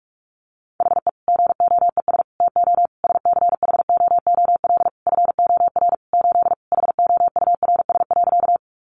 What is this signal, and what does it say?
Morse code, heard as a keyed tone, about 44 wpm, 700 Hz. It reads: HE GOES TO SCHOOL FOR 8 HOURS.
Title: Additional Easy Phrases 45wpm